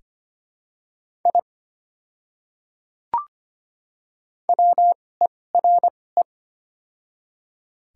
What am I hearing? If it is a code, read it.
Morse code, heard as a keyed tone, about 25 wpm, spaced as for 18 wpm, 700 Hz. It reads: I WERE
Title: 2x Character Spacing - Top 200 Words - 25wpm